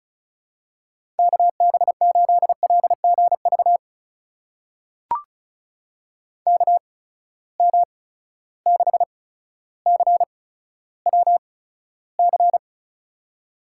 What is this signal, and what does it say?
Morse code, heard as a keyed tone, about 34 wpm, spaced as for 15 wpm, 700 Hz. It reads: KB8LGV KM6CWC
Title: USA Call Signs - 6 Characters - 8x Character Spacing - 35wpm